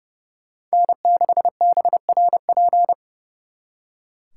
Morse code, heard as a keyed tone, about 35 wpm, 700 Hz. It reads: N6BRP
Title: Speed Racing - USA 1x3 Call Signs 30wpm